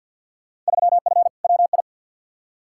Morse code, heard as a keyed tone, about 52 wpm, 700 Hz. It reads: 3F WI
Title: Field Day Exchanges - 50wpm